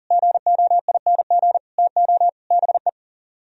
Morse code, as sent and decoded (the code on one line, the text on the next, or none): --. --- .. -. --. / - --- / -... .
GOING TO BE